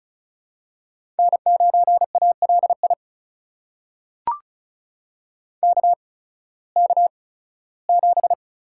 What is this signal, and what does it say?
Morse code, heard as a keyed tone, about 36 wpm, spaced as for 19 wpm, 700 Hz. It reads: N9ALI KK7
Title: USA Call Signs - 8x Character Spacing - 35wpm